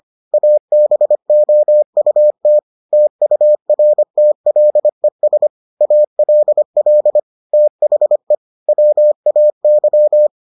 Morse code, heard as a keyed tone, about 25 wpm, 600 Hz. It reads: ABOUT TURTLES ALL THE WAY